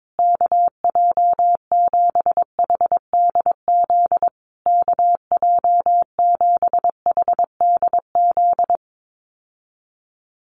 Morse code, heard as a keyed tone, about 22 wpm, 700 Hz. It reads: KJ7HDZ KJ7HDZ
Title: USA Call Signs Repeated 3x 22wpm